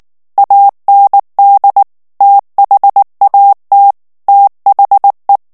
Morse code, heard as a keyed tone, about 19 wpm, 800 Hz. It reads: AND THAT THE
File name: ANGLAIS texte clair 3 vitesse 18 mots minute